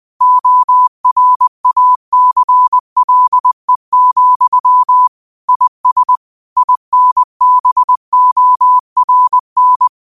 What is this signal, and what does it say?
Morse code, heard as a keyed tone, about 20 wpm, 1000 Hz. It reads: ORACLE, IS INBORN
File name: qotdfiles / 20wpm